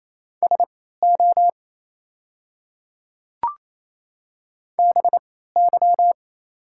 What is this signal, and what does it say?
Morse code, heard as a keyed tone, about 27 wpm, spaced as for 16 wpm, 690 Hz. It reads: SO BY
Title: 3x Character Spacing - Top 200 Words - 28wpm